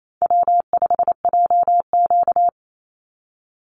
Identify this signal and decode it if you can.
Morse code, heard as a keyed tone, about 28 wpm, 700 Hz. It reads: W5JQ
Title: Easy Call Signs Repeated 2x - 28wpm